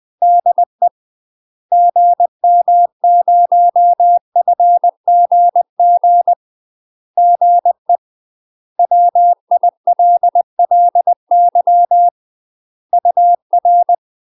Morse code, heard as a keyed tone, about 20 wpm, 700 Hz. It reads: DE GM0FGG GE WILLY UR
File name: UK MASTER QSOs TRAINING / 20WPM / UK MASTER QSOs 0072